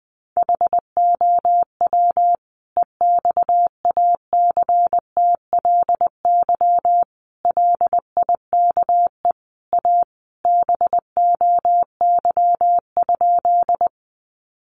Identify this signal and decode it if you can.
Morse code, heard as a keyed tone, about 20 wpm, 700 Hz. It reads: HOW EXACTLY LIKE A BOY?
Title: Sentences from Top 900 Words 20wpm